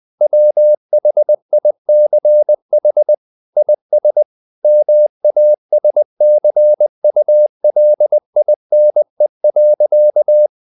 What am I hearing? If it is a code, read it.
Morse code, heard as a keyed tone, about 20 wpm, 600 Hz. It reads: WHICH IS MASCULINE.
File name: the old man and the sea / the old man and the sea part 1 20wpm